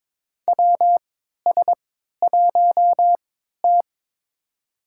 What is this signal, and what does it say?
Morse code, heard as a keyed tone, about 21 wpm, spaced as for 12 wpm, 700 Hz. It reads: WS1T
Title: USA Call Signs - 4 Characters - 3x Character Spacing - 22wpm